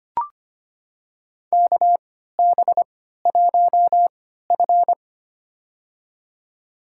Morse code, heard as a keyed tone, about 25 wpm, spaced as for 14 wpm, 700 Hz. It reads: KB1F
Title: USA Call Signs - 4 Characters - 3x Character Spacing - 25wpm